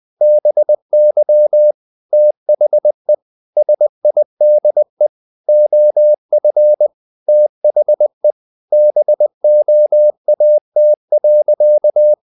Morse code, heard as a keyed tone, about 20 wpm, 600 Hz. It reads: BY THE SIDE OF THE BOAT.
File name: the old man and the sea / the old man and the sea part 1 20wpm